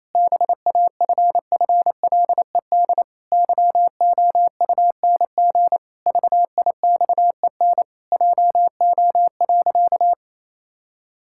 Morse code, heard as a keyed tone, about 28 wpm, 700 Hz. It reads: BAFFLED YOUNG VIXEN JO.
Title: Pangram Challenge v2 - 28wpm